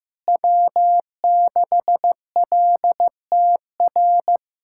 Morse code, heard as a keyed tone, about 15 wpm, 700 Hz. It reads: W6LTR
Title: USA Callsign DE Callsign - 5 Characters - 15wpm